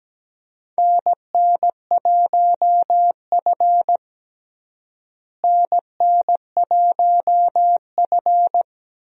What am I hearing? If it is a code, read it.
Morse code, heard as a keyed tone, about 17 wpm, 700 Hz. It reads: NN1F NN1F
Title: USA Call Signs - 4 Characters - Repeated 2x - 17wpm